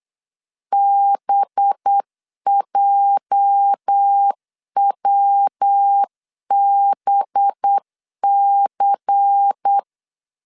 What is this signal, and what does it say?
Morse code, heard as a keyed tone, about 8 wpm, 800 Hz. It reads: BJWBC
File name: CW / DET / 1 9 1500C S A 8wpm 5637